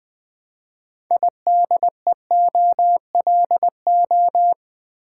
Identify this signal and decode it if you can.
Morse code, heard as a keyed tone, about 20 wpm, 700 Hz. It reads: IDEOLO
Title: Build A Word - 2 Letters At A Time - 8-Letter Words - 20wpm